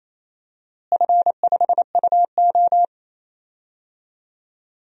Morse code, heard as a keyed tone, about 28 wpm, 700 Hz. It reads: F5UO
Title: DXCC-100 Call Signs - 4 Characters - Repeated 2x - 28wpm